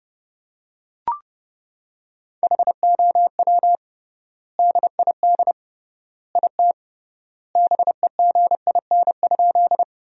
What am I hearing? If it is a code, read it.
Morse code, heard as a keyed tone, about 30 wpm, 700 Hz. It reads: HOW DID IT BEGIN?
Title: 3x Word-Spacing - Sentences from Top 300 Words - 30wpm